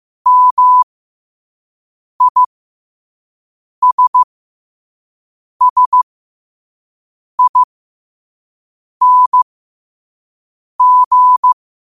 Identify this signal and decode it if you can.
Morse code, heard as a keyed tone, about 15 wpm, spaced as for 5 wpm, 1000 Hz. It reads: MISSING
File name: qotdfiles / 5wpm